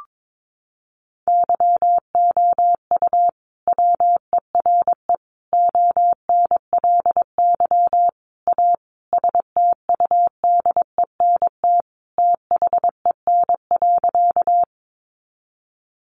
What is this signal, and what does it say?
Morse code, heard as a keyed tone, about 22 wpm, 700 Hz. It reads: YOU WERE ONLY A STUDENT THEN.
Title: Sentences from Top 1000 Words 22wpm